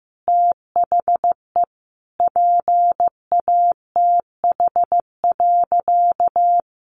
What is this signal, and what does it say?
Morse code, heard as a keyed tone, about 14 wpm, 700 Hz. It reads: THE PATH.
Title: A Sound Of Thunder 15wpm